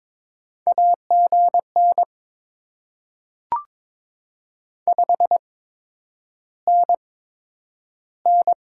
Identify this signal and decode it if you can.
Morse code, heard as a keyed tone, about 22 wpm, spaced as for 9 wpm, 700 Hz. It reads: AGN 5NN
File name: Top 20 QSO Elements - 8x Character Spacing - 22wpm